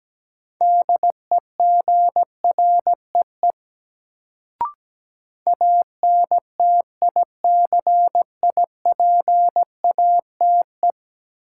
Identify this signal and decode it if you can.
Morse code, heard as a keyed tone, about 17 wpm, 700 Hz. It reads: DEGREE ANTICIPATE
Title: Sentences from Top 1500 Words - New Word Drill - 17wpm